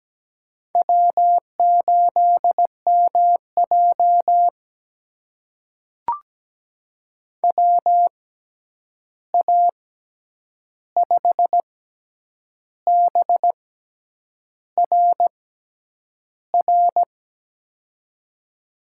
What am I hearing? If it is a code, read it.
Morse code, heard as a keyed tone, about 17 wpm, spaced as for 8 wpm, 700 Hz. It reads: W8MJ WA5BRR
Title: USA Call Signs - 6x Character Spacing - 17wpm